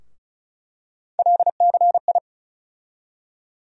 Morse code, heard as a keyed tone, about 36 wpm, 700 Hz. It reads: LCI
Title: USA Call Sign Postfixes - 3 Letters 35wpm